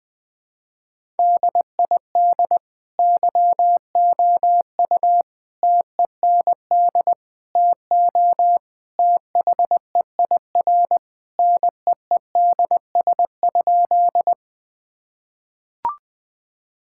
Morse code, heard as a keyed tone, about 20 wpm, 700 Hz. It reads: DID YOU TEND TO THEIR NEEDS?
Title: Sentences from Top 1000 Words - Review Part 5 - 20wpm